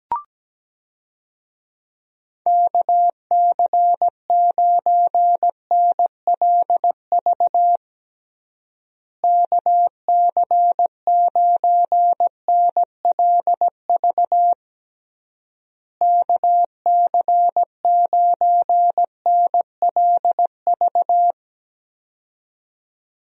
Morse code, heard as a keyed tone, about 17 wpm, 700 Hz. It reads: KC9NLV KC9NLV KC9NLV
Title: USA Call Signs - 6 Characters - Repeated 3x - 17wpm